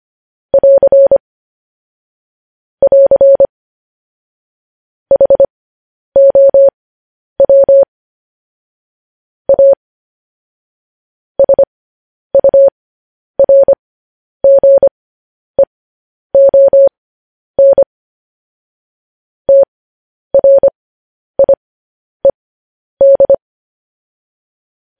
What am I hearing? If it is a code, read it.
Morse code, heard as a keyed tone, about 25 wpm, spaced as for 10 wpm, 550 Hz. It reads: <AR> <AR> HOW A SURGEON TRIED
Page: news headlines in morse code at 10 wpm